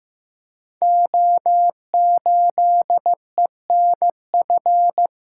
Morse code, heard as a keyed tone, about 15 wpm, 700 Hz. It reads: O8ENF
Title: Code Groups - 5 Letter:Numbers - 15wpm